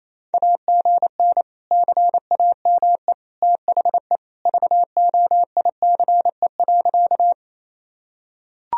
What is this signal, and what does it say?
Morse code, heard as a keyed tone, about 28 wpm, 700 Hz. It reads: AGN CAME THE VOICE.
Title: Sentences from Top 500 Words and Abbreviations 28wpm